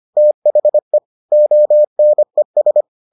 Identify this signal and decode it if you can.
Morse code, heard as a keyed tone, about 25 wpm, 600 Hz. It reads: THE ONES